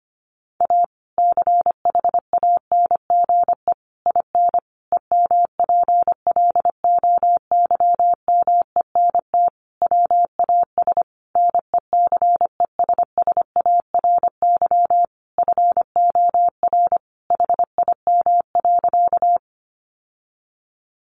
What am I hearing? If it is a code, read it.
Morse code, heard as a keyed tone, about 25 wpm, 700 Hz. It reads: A CHANGE IN EMPLOYMENT WAS NECESSARY FOR HIM.
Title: Sentences from Top 1500 Words 25wpm